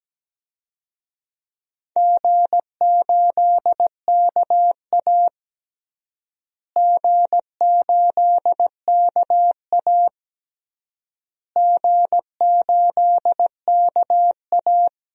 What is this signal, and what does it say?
Morse code, heard as a keyed tone, about 17 wpm, 700 Hz. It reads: G8KA G8KA G8KA
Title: DXCC-100 Call Signs - 4 Characters - Repeated 3x - 17wpm